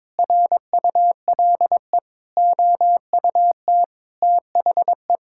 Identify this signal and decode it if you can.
Morse code, heard as a keyed tone, about 22 wpm, 700 Hz. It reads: RULE OUT THE
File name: practice / 22wpm / Sets of 8 Words - 22wpm